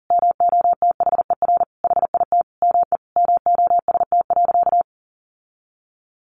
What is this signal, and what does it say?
Morse code, heard as a keyed tone, about 40 wpm, 700 Hz. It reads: MOTHER HIT ME MOST.
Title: Sentences from Top 1400 Words 40wpm